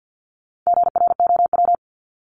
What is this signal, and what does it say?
Morse code, heard as a keyed tone, about 50 wpm, 700 Hz. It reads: GROW
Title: Top 300 Words Repeated 50wpm